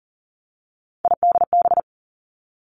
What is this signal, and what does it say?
Morse code, heard as a keyed tone, about 40 wpm, 700 Hz. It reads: IDB
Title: Three Challenging Letter-Number Combinations 40wpm